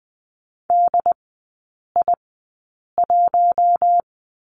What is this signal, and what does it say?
Morse code, heard as a keyed tone, about 20 wpm, spaced as for 8 wpm, 700 Hz. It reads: DI1
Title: 074 Lesson 13e - Letter L - Three Characters 20wpm